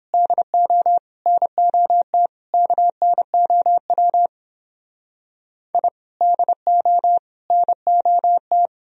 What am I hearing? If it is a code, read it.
Morse code, heard as a keyed tone, about 31 wpm, 700 Hz. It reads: DO NOT KNOW I DO NOT